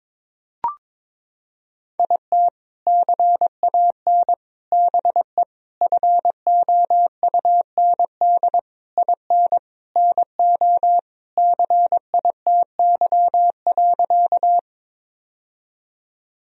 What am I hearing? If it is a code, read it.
Morse code, heard as a keyed tone, about 22 wpm, 700 Hz. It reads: IT CAN BE FOUND IN NO CITY.
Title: Sentences from Top 1000 Words - Review Part 1 - 22wpm